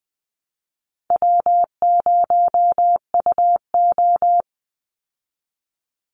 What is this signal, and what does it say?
Morse code, heard as a keyed tone, about 20 wpm, 700 Hz. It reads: W0UO
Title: Call Signs Repeated 2x - 20wpm